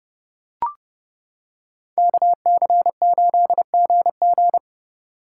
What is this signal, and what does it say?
Morse code, heard as a keyed tone, about 30 wpm, 700 Hz. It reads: KC8GG